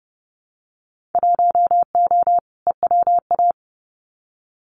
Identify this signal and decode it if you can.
Morse code, heard as a keyed tone, about 30 wpm, 700 Hz. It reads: 1O EWA
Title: Winter Field Day Exchanges Encore 30wpm